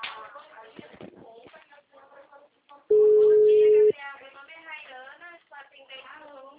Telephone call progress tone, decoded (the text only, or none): ringback tone